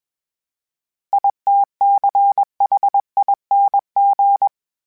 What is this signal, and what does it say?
Morse code, heard as a keyed tone, about 21 wpm, 800 Hz. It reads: ITCHING
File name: Special INGs / Special INGs-mp3s / Sngl Ing @20